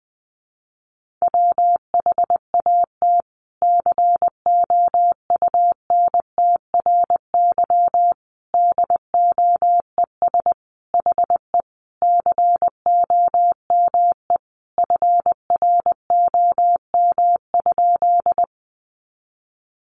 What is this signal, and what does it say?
Morse code, heard as a keyed tone, about 20 wpm, 700 Hz. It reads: WHAT COUNTRY DOES HE COME FROM?
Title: Sentences from Top 300 Words 20wpm